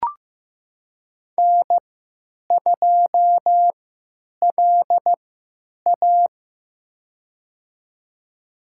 Morse code, heard as a keyed tone, about 15 wpm, spaced as for 9 wpm, 700 Hz. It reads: N2LA
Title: USA Call Signs - 4 Characters - 3x Character Spacing - 15wpm